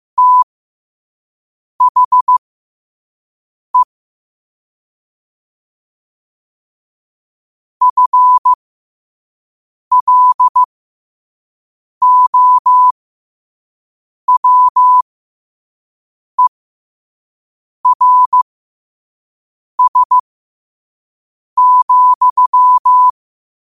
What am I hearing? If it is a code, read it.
Morse code, heard as a keyed tone, about 15 wpm, spaced as for 5 wpm, 1000 Hz. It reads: THE FLOWERS,
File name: qotdfiles / 5wpm